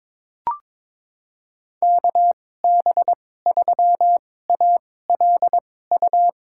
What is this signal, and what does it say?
Morse code, heard as a keyed tone, about 22 wpm, spaced as for 16 wpm, 700 Hz. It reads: KB3ALU
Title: USA Call Signs - 6 Characters - 2x Character Spacing - 22wpm